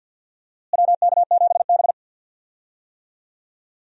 Morse code, heard as a keyed tone, about 47 wpm, 700 Hz. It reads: WK7B
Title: USA Call Signs - 4 Characters - Repeated 3x - 50wpm